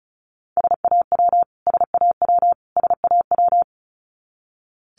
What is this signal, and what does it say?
Morse code, heard as a keyed tone, about 34 wpm, 700 Hz. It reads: SAW SAW SAW
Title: Top 500 Words Repeated 35wpm